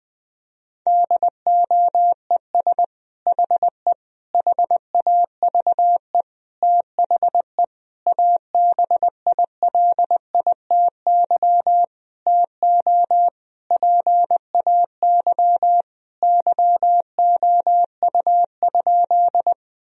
Morse code, heard as a keyed tone, about 20 wpm, 700 Hz. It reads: DOES HE HAVE THE ABILITY TO PAY YOU?
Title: Sentences from Top 1000 Words - Review Part 3 - 20wpm